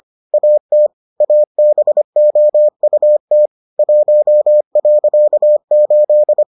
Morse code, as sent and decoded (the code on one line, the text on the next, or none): .- - / .- -... --- ..- - / .---- .-.-.- ---..
AT ABOUT 1.8